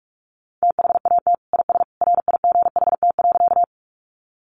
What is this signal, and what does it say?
Morse code, heard as a keyed tone, about 44 wpm, 700 Hz. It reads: THAT IS RIGHT.
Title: Sentences from Top 200 Words 45wpm